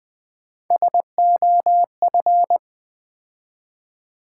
Morse code, heard as a keyed tone, about 20 wpm, 700 Hz. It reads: SOF
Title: Build A Word - 3 Letters At A Time - 8-Letter Words - 20wpm